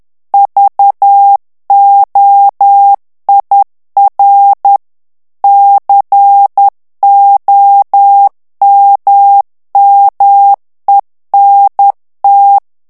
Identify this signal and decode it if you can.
Morse code, heard as a keyed tone, about 10 wpm, 800 Hz. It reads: VOIR COMMENT
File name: FRANCAIS TEXTE CLAIR 2 VITESSE 10 MOTS MINUTES